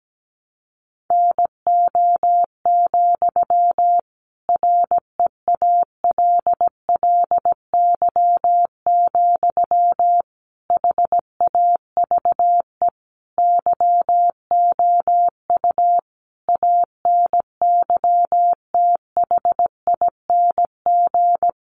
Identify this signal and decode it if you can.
Morse code, heard as a keyed tone, about 17 wpm, 700 Hz. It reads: NO, REALLY, HAVE YOU ANYTHING
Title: Sentences from Top 1000 Words 17wpm